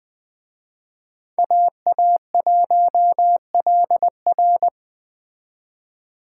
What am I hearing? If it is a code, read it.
Morse code, heard as a keyed tone, about 19 wpm, 700 Hz. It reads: AA1LR